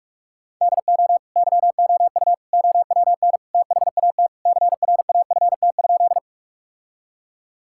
Morse code, heard as a keyed tone, about 46 wpm, 700 Hz. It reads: DO YOU OWN THAT CRAFT?
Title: Sentences from Top 1000 Words - Review Part 6 - 45wpm